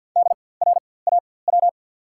Morse code, heard as a keyed tone, about 53 wpm, spaced as for 24 wpm, 690 Hz. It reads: DRAW
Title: 4x Character Spacing - Top 500 Words - 50wpm